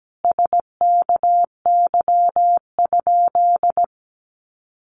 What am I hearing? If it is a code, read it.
Morse code, heard as a keyed tone, about 17 wpm, 700 Hz. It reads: SKY?
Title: Sentences from Top 1200 Words 17wpm